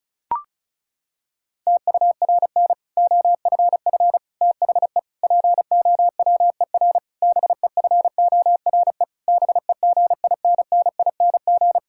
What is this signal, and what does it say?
Morse code, heard as a keyed tone, about 36 wpm, 700 Hz. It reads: TURN OFF THE POWER BEFORE BEGINNING
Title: Sentences from Top 1000 Words - Review Part 6 - 35wpm